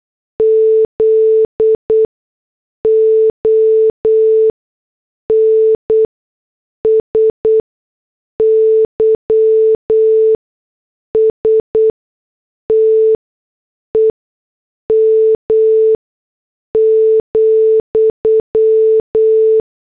Morse code, heard as a keyed tone, about 8 wpm, spaced as for 6 wpm, 440 Hz. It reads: ZON SYSTEM,